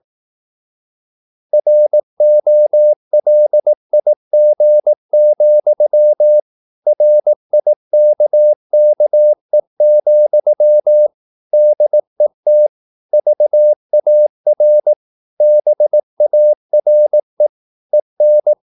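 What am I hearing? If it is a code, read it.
Morse code, heard as a keyed tone, about 18 wpm, 600 Hz. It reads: ROLIG, RIKKE, DET VAR BARE EN
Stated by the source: Slow (QRS) Morse practice at 18 wpm, CW med Gnister.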